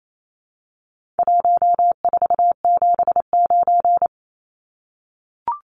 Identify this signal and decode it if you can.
Morse code, heard as a keyed tone, about 28 wpm, 700 Hz. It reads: 1479
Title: Numbers - Contest Style - 4-Digit 28wpm